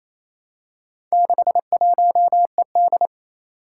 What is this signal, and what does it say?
Morse code, heard as a keyed tone, about 27 wpm, 700 Hz. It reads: 61ED
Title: Four Challenging Letter-Number Combinations 28wpm